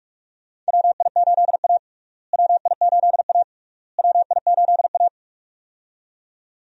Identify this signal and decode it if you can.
Morse code, heard as a keyed tone, about 46 wpm, 700 Hz. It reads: WI8A WI8A WI8A
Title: USA Call Signs - 4 Characters - Repeated 3x - 45wpm